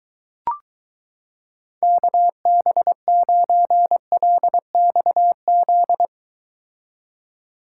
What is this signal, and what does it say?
Morse code, heard as a keyed tone, about 22 wpm, 700 Hz. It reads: KB9LXZ